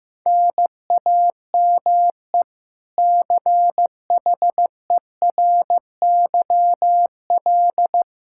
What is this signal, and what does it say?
Morse code, heard as a keyed tone, about 16 wpm, 700 Hz. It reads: NAME CHERYL
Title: K1USN SST QSO Practice 15wpm